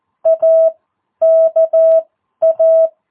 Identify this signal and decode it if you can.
Morse code, heard as a keyed tone, about 14 wpm, spaced as for 10 wpm, 650 Hz. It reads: AKA
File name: jh8cbh / 2022cw